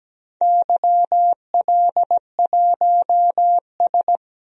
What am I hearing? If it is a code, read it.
Morse code, heard as a keyed tone, about 17 wpm, 700 Hz. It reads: YL1S
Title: DXCC-100 Call Signs - 4 Characters - 17wpm